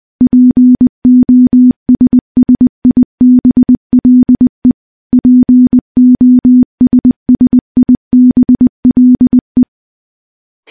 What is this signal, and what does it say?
Morse code, heard as a keyed tone, about 20 wpm, 250 Hz. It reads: POSSIBLE POSSIBLE
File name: morse / morse-training-20WPM